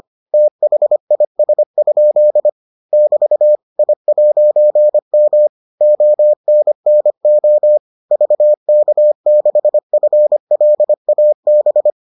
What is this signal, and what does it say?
Morse code, heard as a keyed tone, about 25 wpm, 600 Hz. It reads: THIS? <BT> I'M ONNO VK6FLAB